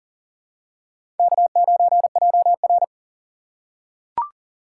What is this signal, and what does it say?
Morse code, heard as a keyed tone, about 40 wpm, 700 Hz. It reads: K9JR
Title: USA Call Signs - 4 Characters - 2x Character Spacing - 40wpm